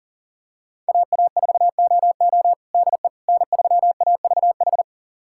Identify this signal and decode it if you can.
Morse code, heard as a keyed tone, about 40 wpm, 700 Hz. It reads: AA4OO DE N3AVH
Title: USA Callsign DE Callsign - 5 Characters - 40wpm